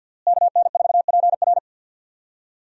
Morse code, heard as a keyed tone, about 52 wpm, 700 Hz. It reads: KN4PR
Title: USA Call Signs - 5 Characters - Repeated 2x - 50wpm